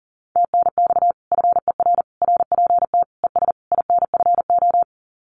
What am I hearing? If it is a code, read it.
Morse code, heard as a keyed tone, about 43 wpm, 700 Hz. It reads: TNX FER RPT ES INFO
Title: Common QSO Phrases Encore 40wpm